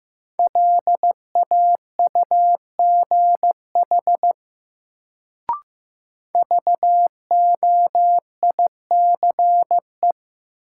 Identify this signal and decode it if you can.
Morse code, heard as a keyed tone, about 15 wpm, 700 Hz. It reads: LAUGH VOICE
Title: Sentences from Top 500 Words - New Word Drill - 15wpm